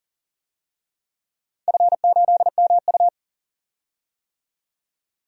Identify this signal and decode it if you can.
Morse code, heard as a keyed tone, about 42 wpm, 700 Hz. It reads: F8MU
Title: DXCC-100 Call Signs - 4 Characters - 40wpm